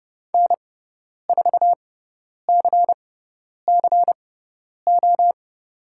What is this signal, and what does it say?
Morse code, heard as a keyed tone, about 30 wpm, spaced as for 10 wpm, 700 Hz. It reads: N4CCO
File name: Lesson 17g - Letter u - Call Signs 30wpm